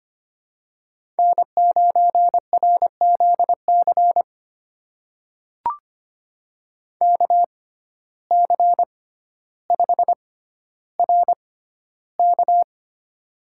Morse code, heard as a keyed tone, about 25 wpm, spaced as for 13 wpm, 700 Hz. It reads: N9RZC KC5RK
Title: USA Call Signs - 5 Characters - 6x Character Spacing - 25wpm